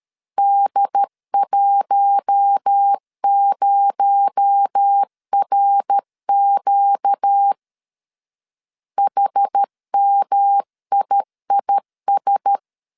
Morse code, heard as a keyed tone, about 13 wpm, 800 Hz. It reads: D10RQ HMIIS
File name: CW / DET / 1 6 1500C S B 12wpm 3431